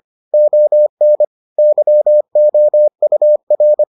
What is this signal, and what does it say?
Morse code, heard as a keyed tone, about 25 wpm, 600 Hz. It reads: ON YOUR